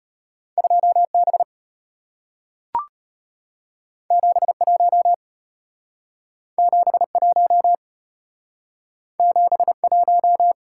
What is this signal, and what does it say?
Morse code, heard as a keyed tone, about 34 wpm, 700 Hz. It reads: 2B 71 71 71